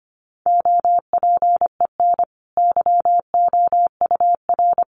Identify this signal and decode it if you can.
Morse code, heard as a keyed tone, about 25 wpm, 700 Hz. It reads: OPEN YOUR
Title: Sentences from Top 1500 Words 25wpm